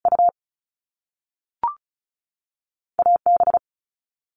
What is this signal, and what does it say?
Morse code, heard as a keyed tone, about 36 wpm, 700 Hz. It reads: U A B
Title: Top Two Letter Combinations in Words 35wpm